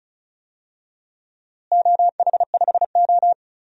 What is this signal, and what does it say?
Morse code, heard as a keyed tone, about 34 wpm, 700 Hz. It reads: OH5O
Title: DXCC-100 Call Signs - 4 Characters - Repeated 3x - 35wpm